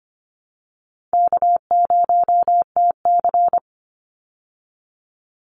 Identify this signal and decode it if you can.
Morse code, heard as a keyed tone, about 24 wpm, 700 Hz. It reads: K0TC
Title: Easy Call Signs Repeated 3x - 25wpm